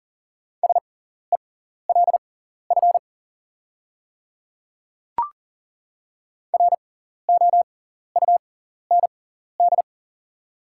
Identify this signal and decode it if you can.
Morse code, heard as a keyed tone, about 40 wpm, spaced as for 14 wpm, 690 Hz. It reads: SELF ROUND
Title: 6x Character Spacing - Top 200 Words - 40wpm